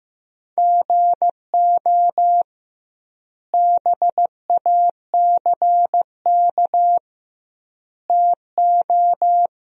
Morse code, heard as a keyed tone, about 15 wpm, 700 Hz. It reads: GO BACK TO
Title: 2x Word-Spacing - Sentences from Top 500 Words - 15wpm